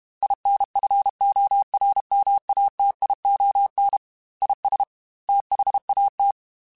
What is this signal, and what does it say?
Morse code, heard as a keyed tone, about 32 wpm, 800 Hz. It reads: INFORMATION IS THAT
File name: Technical Subjects / Technical Subjects-mp3s / Sw Pwr Sup @30